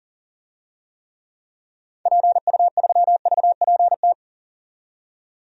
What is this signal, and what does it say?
Morse code, heard as a keyed tone, about 40 wpm, 700 Hz. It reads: PU3VPT
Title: DXCC-100 Call Signs - 6 Characters - 40wpm